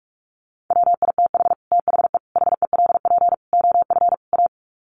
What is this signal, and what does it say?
Morse code, heard as a keyed tone, about 44 wpm, 700 Hz. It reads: WITH THE HELP OF A
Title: Sets of 5 Words 45wpm